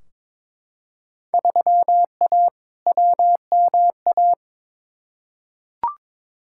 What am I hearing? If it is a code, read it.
Morse code, heard as a keyed tone, about 22 wpm, 700 Hz. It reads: 3A WMA
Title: Field Day Exchanges - 22wpm